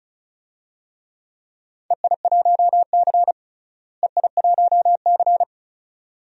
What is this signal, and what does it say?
Morse code, heard as a keyed tone, about 36 wpm, 700 Hz. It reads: EI1C EI1C
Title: DXCC-100 Call Signs - 4 Characters - Repeated 2x - 35wpm